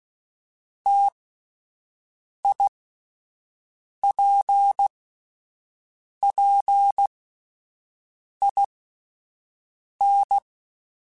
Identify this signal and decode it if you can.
Morse code, heard as a keyed tone, about 16 wpm, 800 Hz. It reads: T I P P I N
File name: Difficult Combos / Difficult Combos-mp3s / T AOEI @10